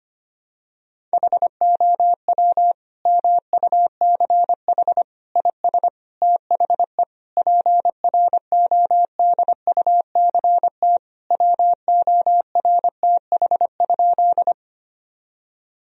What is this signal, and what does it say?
Morse code, heard as a keyed tone, about 25 wpm, 700 Hz. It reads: HOW MUCH IS THE PRODUCT WORTH?
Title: Sentences from Top 1000 Words - Review Part 2 - 25wpm